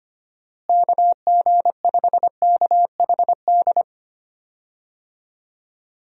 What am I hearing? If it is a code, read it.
Morse code, heard as a keyed tone, about 25 wpm, 700 Hz. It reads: KG5KHD
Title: USA Call Signs - 6 Characters - Repeated 3x - 25wpm